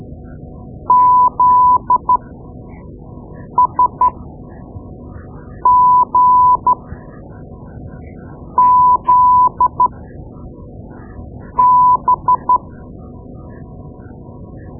Morse code, heard as a keyed tone, about 12 wpm, spaced as for 4 wpm, 1000 Hz. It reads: ZSGZB
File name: Morse Lessons / Morse Lessons / Tape-2B